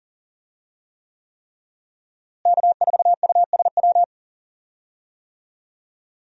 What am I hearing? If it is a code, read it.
Morse code, heard as a keyed tone, about 37 wpm, 700 Hz. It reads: K4USW
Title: USA Call Signs - 5 Characters - Repeated 3x - 40wpm